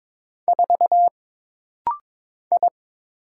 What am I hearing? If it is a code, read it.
Morse code, heard as a keyed tone, about 22 wpm, 700 Hz. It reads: 4 I
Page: This rapid-fire timing is geared to push your ICR (Instant Character Recognition) skills.